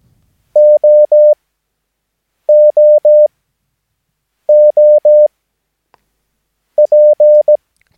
Morse code, heard as a keyed tone, about 17 wpm, spaced as for 6 wpm, 600 Hz. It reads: OOOP
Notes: Övningsmaterial med alfabetet A till Ö samt siffror 0 till 9 enligt International/Continental Morse Code.